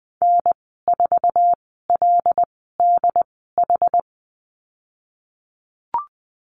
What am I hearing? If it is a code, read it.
Morse code, heard as a keyed tone, about 19 wpm, spaced as for 14 wpm, 700 Hz. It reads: N4LDH
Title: 122 Lesson 20g - Number 3 - Call Signs 20wpm